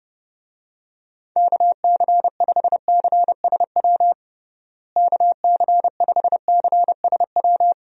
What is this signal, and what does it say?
Morse code, heard as a keyed tone, about 30 wpm, 700 Hz. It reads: KC5CSW KC5CSW
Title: USA Call Signs - 6 Characters - Repeated 3x - 30wpm